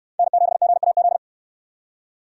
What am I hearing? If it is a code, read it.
Morse code, heard as a keyed tone, about 69 wpm, 700 Hz. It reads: N7GAZ